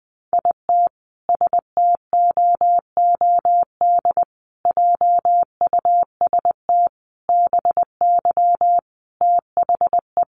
Morse code, heard as a keyed tone, about 20 wpm, 700 Hz. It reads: IT STOOD JUST BY THE
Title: Sentences from Top 1400 Words 20wpm